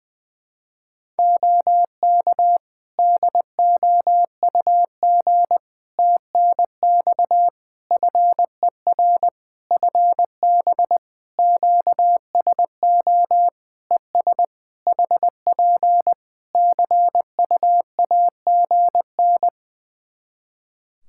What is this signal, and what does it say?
Morse code, heard as a keyed tone, about 21 wpm, 700 Hz. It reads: OK DOUG TNX FER FB QSO ES HP CUAGN
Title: Practice QSO - Exchange 4 - Ending - 20wpm